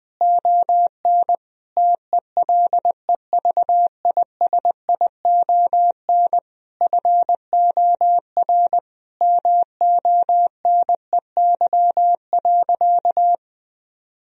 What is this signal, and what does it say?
Morse code, heard as a keyed tone, about 20 wpm, 700 Hz. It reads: ON TELEVISION FOR MONEY.
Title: Sentences from Top 1000 Words - Review Part 4 - 20wpm